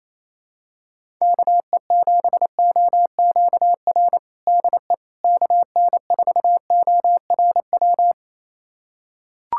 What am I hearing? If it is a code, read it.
Morse code, heard as a keyed tone, about 28 wpm, 700 Hz. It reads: KE7OQR DE KN4ORW
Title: USA Callsign DE Callsign - 6 Characters - 28wpm